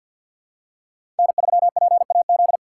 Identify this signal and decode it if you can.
Morse code, heard as a keyed tone, about 47 wpm, 700 Hz. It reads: N3PAZ
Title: USA Call Signs - 3x Character Spacing - 50wpm